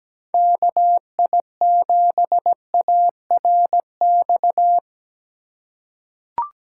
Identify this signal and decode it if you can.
Morse code, heard as a keyed tone, about 17 wpm, 700 Hz. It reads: KI7ARX
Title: USA Call Signs - 6 Characters - 4x Character Spacing - 17wpm